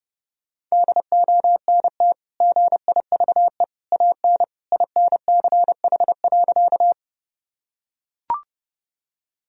Morse code, heard as a keyed tone, about 30 wpm, 700 Hz. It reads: DONT GIVE AN INCH.
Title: Sentences from Top 1000 Words - Review Part 3 - 30wpm